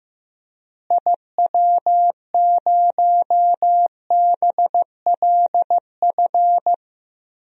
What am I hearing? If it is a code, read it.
Morse code, heard as a keyed tone, about 15 wpm, 700 Hz. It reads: IW0BLF
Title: DXCC-100 Call Signs - 6 Characters - Repeated 3x - 15wpm